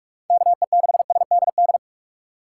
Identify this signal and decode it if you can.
Morse code, heard as a keyed tone, about 44 wpm, 700 Hz. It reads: KE6SDD
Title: USA Call Signs - 6 Characters - 2x Character Spacing - 45wpm